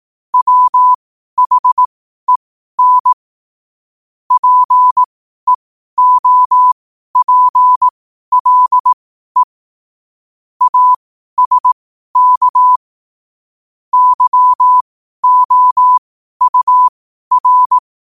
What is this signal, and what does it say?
Morse code, heard as a keyed tone, about 18 wpm, spaced as for 12 wpm, 1000 Hz. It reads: WHEN PEOPLE ASK YOUR
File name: qotdfiles / 12wpm